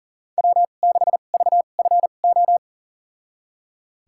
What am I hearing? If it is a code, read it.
Morse code, heard as a keyed tone, about 40 wpm, spaced as for 29 wpm, 700 Hz. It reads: W6VFO
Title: USA Call Signs - 5 Characters - 2x Character Spacing - 40wpm